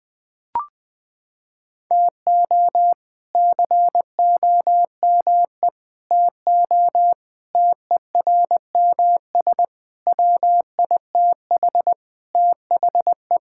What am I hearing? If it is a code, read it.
Morse code, heard as a keyed tone, about 20 wpm, 700 Hz. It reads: TO COME TO TERMS WITH THE
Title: Sentences with 6-Word Phrases - 20wpm